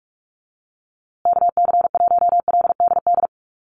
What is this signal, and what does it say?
Morse code, heard as a keyed tone, about 44 wpm, 700 Hz. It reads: KC1LDD
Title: USA Call Signs Repeated 2x 45wpm